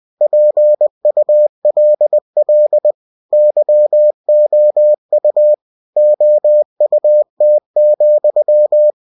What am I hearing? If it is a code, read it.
Morse code, heard as a keyed tone, about 20 wpm, 600 Hz. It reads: PULL YOU OUT,
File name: cw-books / winnie-the-pooh / winnie-20wpm-page-0018